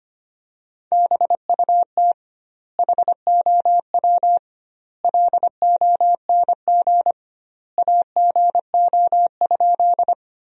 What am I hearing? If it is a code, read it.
Morse code, heard as a keyed tone, about 25 wpm, 700 Hz. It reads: BUT HOW LONG AGO?
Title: 2x Word-Spacing - Sentences from Top 500 Words - 25wpm